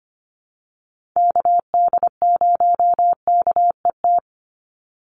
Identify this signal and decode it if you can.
Morse code, heard as a keyed tone, about 25 wpm, 700 Hz. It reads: KD0KET
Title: USA Call Signs 25wpm